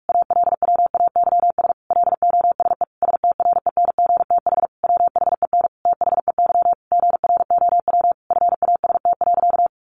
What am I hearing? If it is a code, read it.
Morse code, heard as a keyed tone, about 44 wpm, 700 Hz. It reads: ALWAYS LOSE STRENGTH WHEN THEY GROW FAST.
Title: Sentences from Top 1300 Words 45wpm